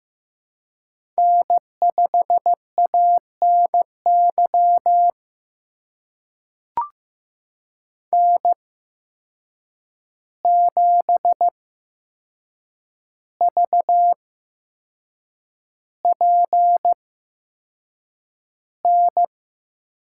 Morse code, heard as a keyed tone, about 15 wpm, spaced as for 6 wpm, 700 Hz. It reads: N5ANY N7VPN
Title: USA Call Signs - 5 Characters - 8x Character Spacing - 15wpm